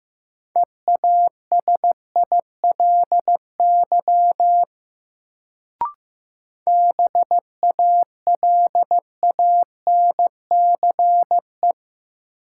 Morse code, heard as a keyed tone, about 16 wpm, 700 Hz. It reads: EASILY BALANCE
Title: Sentences from Top 900 Words - New Word Drill - 15wpm